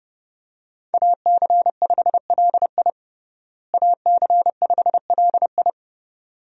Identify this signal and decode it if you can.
Morse code, heard as a keyed tone, about 32 wpm, 700 Hz. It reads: AC5LI AC5LI
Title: USA Call Signs - 5 Characters - Repeated 2x - 30wpm